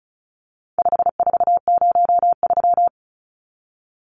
Morse code, heard as a keyed tone, about 34 wpm, 700 Hz. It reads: 5403
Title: Numbers - 4-Digit 35wpm